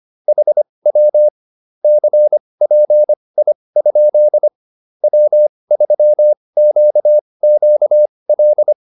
Morse code, heard as a keyed tone, about 25 wpm, spaced as for 20 wpm, 600 Hz. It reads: HW CPI? W3QQL
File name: QSO309 20